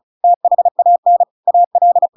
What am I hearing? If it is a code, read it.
Morse code, heard as a keyed tone, about 34 wpm, 700 Hz. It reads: THAN AL
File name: Pinocchio / Pinocchio ch8 35 35 ik1hge